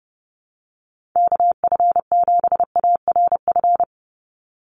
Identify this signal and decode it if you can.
Morse code, heard as a keyed tone, about 30 wpm, 700 Hz. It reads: KF7ARF
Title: USA Call Signs 30wpm